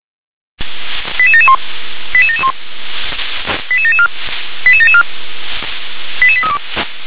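Five-tone selective call (ZVEI), heard as quick runs of five tones, tones of about 70 ms. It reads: 88771 88771 88774 88774 88832
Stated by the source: Received on an SDR.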